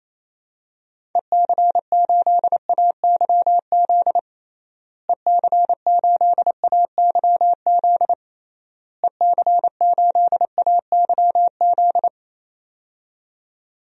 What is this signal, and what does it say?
Morse code, heard as a keyed tone, about 28 wpm, 700 Hz. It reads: EC8AYZ EC8AYZ EC8AYZ
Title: DXCC-100 Call Signs - 6 Characters - Repeated 3x - 28wpm